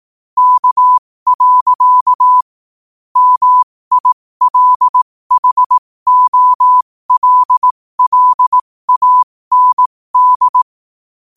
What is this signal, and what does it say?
Morse code, heard as a keyed tone, about 18 wpm, spaced as for 15 wpm, 1000 Hz. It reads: K. MILHOLLAND
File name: qotdfiles / 15wpm